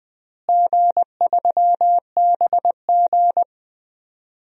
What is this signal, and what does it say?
Morse code, heard as a keyed tone, about 20 wpm, 700 Hz. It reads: G3BG
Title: DXCC-100 Call Signs - 4 Characters - Repeated 2x - 20wpm